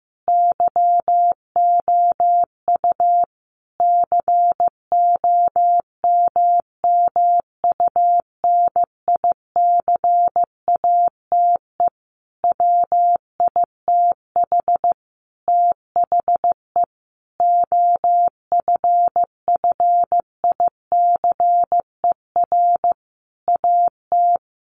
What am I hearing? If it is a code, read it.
Morse code, heard as a keyed tone, about 15 wpm, 700 Hz. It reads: YOU COMMUNICATE WITH THE OFFICER AT
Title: Sentences from Top 1300 Words 15wpm